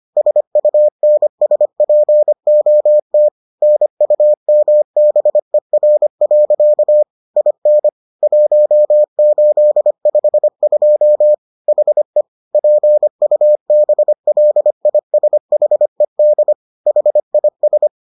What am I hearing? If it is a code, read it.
Morse code, heard as a keyed tone, about 25 wpm, 600 Hz. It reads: SUNSPOT NUMBER. IN 1852 HE PUBLISHED HIS